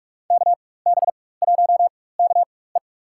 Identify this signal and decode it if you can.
Morse code, heard as a keyed tone, about 42 wpm, spaced as for 21 wpm, 700 Hz. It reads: KB1KE
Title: USA Call Signs - 5 Characters - 4x Character Spacing - 45wpm